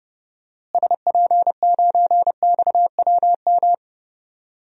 Morse code, heard as a keyed tone, about 30 wpm, 700 Hz. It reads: SP9XWM
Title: DXCC-100 Call Signs - 6 Characters - Repeated 3x - 30wpm